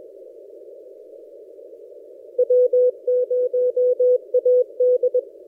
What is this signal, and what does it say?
Morse code, heard as a keyed tone, about 21 wpm, 500 Hz. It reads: W0AD
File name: posts / a35jr-12cw- -2336z